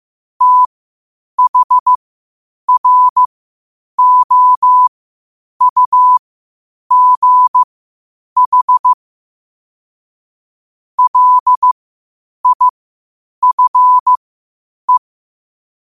Morse code, heard as a keyed tone, about 15 wpm, spaced as for 8 wpm, 1000 Hz. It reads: THROUGH LIFE
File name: qotdfiles / 8wpm